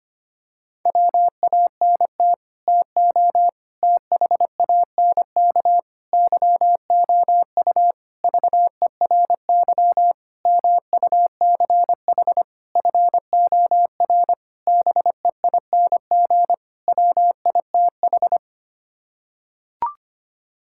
Morse code, heard as a keyed tone, about 25 wpm, 700 Hz. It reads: WANT TO THANK YOU VERY MUCH FOR BEING WITH
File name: Sets of 9 Words - 25wpm